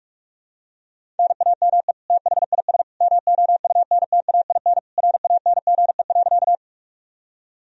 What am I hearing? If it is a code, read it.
Morse code, heard as a keyed tone, about 45 wpm, 700 Hz. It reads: NAME THIS MOUNTAIN RANGE.
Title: Sentences from Top 1000 Words - Review Part 4 - 45wpm